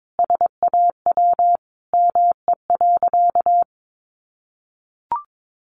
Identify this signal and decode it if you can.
Morse code, heard as a keyed tone, about 22 wpm, 700 Hz. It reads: SAW ME.
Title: Sentences from Top 1100 Words 22wpm